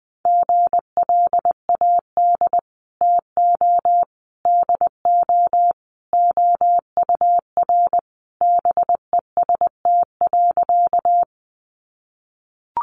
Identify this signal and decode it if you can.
Morse code, heard as a keyed tone, about 20 wpm, 700 Hz. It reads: GLAD TO DO OUR BEST.
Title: Sentences from Top 800 Words 20wpm